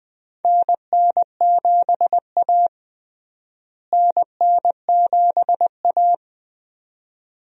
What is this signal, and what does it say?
Morse code, heard as a keyed tone, about 20 wpm, 700 Hz. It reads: NN7A NN7A
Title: USA Call Signs - 4 Characters - Repeated 2x - 20wpm